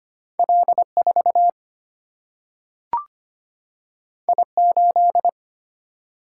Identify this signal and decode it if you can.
Morse code, heard as a keyed tone, about 25 wpm, 700 Hz. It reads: L 4 I 8
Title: Single-Letter Single-Number 25wpm